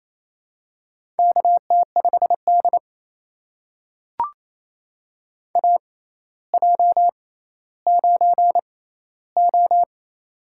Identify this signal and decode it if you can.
Morse code, heard as a keyed tone, about 28 wpm, spaced as for 14 wpm, 700 Hz. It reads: KT5D AJ9O
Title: USA Call Signs - 4 Characters - 6x Character Spacing - 28wpm